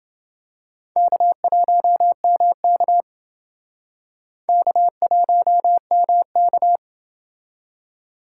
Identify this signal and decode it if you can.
Morse code, heard as a keyed tone, about 28 wpm, 700 Hz. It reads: K1MK K1MK